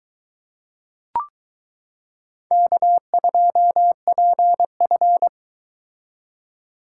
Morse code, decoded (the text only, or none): K2PF